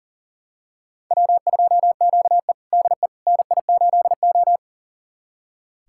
Morse code, heard as a keyed tone, about 40 wpm, 700 Hz. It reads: W2QE DE NI8O
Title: USA Callsign DE Callsign - 4 Characters - 40wpm